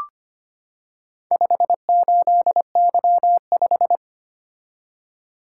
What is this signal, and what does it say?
Morse code, heard as a keyed tone, about 25 wpm, 700 Hz. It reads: 58Y5
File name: practice / 25wpm / Four Letter-Number Combinations - 25wpm